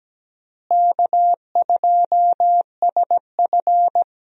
Morse code, heard as a keyed tone, about 17 wpm, 700 Hz. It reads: K2SF
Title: USA Call Signs - 4 Characters - Repeated 3x - 17wpm